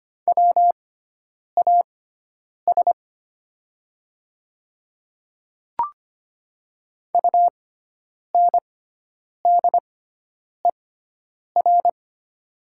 Morse code, heard as a keyed tone, about 25 wpm, spaced as for 9 wpm, 700 Hz. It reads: WAS UNDER
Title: 6x Character Spacing - Top 200 Words - 25wpm